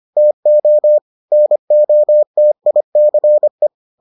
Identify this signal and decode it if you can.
Morse code, heard as a keyed tone, about 25 wpm, 600 Hz. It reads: TO NOTICE